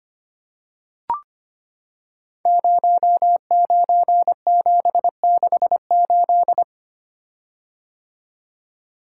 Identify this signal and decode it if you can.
Morse code, heard as a keyed tone, about 25 wpm, 700 Hz. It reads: 09768
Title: Code Groups - 5 Letter:Numbers - 25wpm